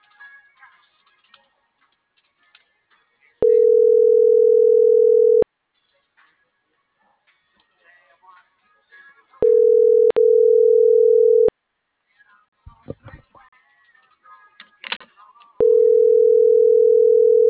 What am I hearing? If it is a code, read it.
Ringback tone (North America), 440 + 480 Hz.
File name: CENTRAL NMP / out itzel.arias